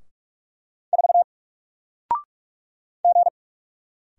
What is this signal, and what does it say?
Morse code, heard as a keyed tone, about 44 wpm, 700 Hz. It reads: 4 G
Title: Single Character (Letter:Number:Punctuation:Prosign) 45wpm